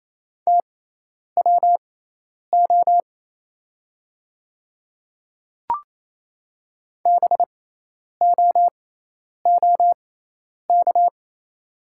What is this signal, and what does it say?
Morse code, heard as a keyed tone, about 28 wpm, spaced as for 10 wpm, 700 Hz. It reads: TWO BOOK
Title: 6x Character Spacing - Top 300 Words - 28wpm